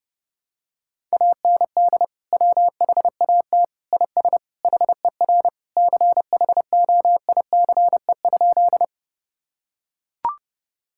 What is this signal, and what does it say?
Morse code, heard as a keyed tone, about 30 wpm, 700 Hz. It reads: AND WHAT IS HER CHOICE?
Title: Sentences from Top 1000 Words - Review Part 6 - 30wpm